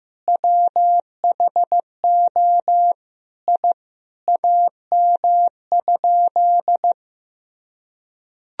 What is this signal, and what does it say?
Morse code, heard as a keyed tone, about 15 wpm, 700 Hz. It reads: WHO I AM?
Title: Sentences from Top 1000 Words - Review Part 2 - 15wpm